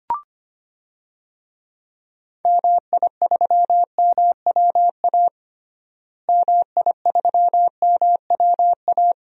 Morse code, heard as a keyed tone, about 25 wpm, 700 Hz. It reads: MI3MWA MI3MWA
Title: DXCC-100 Call Signs - 6 Characters - Repeated 3x - 25wpm